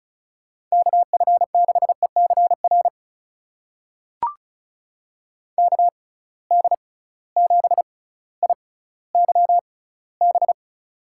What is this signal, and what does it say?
Morse code, heard as a keyed tone, about 36 wpm, spaced as for 18 wpm, 700 Hz. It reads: KF6ECR KD7IYB
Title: USA Call Signs - 6 Characters - 6x Character Spacing - 35wpm